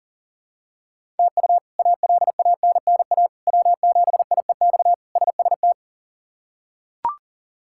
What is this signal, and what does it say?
Morse code, heard as a keyed tone, about 40 wpm, 700 Hz. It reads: TU ALANNA W7IEX SST
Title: K1USN SST QSO Practice - 40wpm